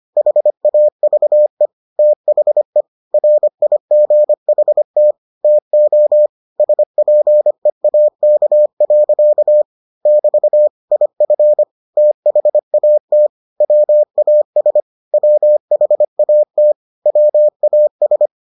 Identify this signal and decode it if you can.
Morse code, heard as a keyed tone, about 25 wpm, 600 Hz. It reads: HAVE THE RIGHT TO SPEAK. <BT> IF THAT WAS WHAT WAS